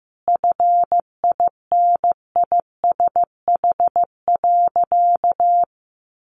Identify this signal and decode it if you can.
Morse code, heard as a keyed tone, about 15 wpm, 700 Hz. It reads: FINISH.
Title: Sentences from Top 1200 Words 15wpm